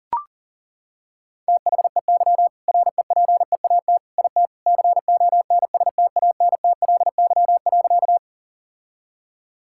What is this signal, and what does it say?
Morse code, heard as a keyed tone, about 40 wpm, 700 Hz. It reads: THEY REPEAT IT CONSTANTLY.
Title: Sentences from Top 1600 Words 40wpm